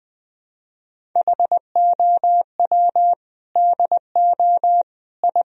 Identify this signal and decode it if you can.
Morse code, heard as a keyed tone, about 20 wpm, 700 Hz. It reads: HOW DO I
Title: Sentences from Top 1000 Words - Review Part 1 - 20wpm